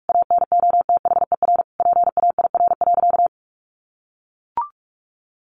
Morse code, heard as a keyed tone, about 44 wpm, 700 Hz. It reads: ANOTHER PAIR.
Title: Sentences from Top 1200 Words 45wpm